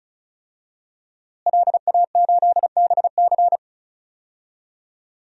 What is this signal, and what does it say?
Morse code, heard as a keyed tone, about 36 wpm, 700 Hz. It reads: LA8BC
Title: DXCC-100 Call Signs - 5 Characters - Repeated 2x - 35wpm